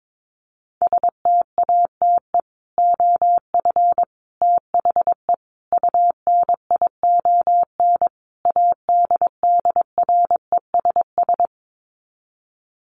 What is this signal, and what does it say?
Morse code, heard as a keyed tone, about 22 wpm, 700 Hz. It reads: STATE OF THE UNION ADDRESS
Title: Sets of 5 Words Encore 22wpm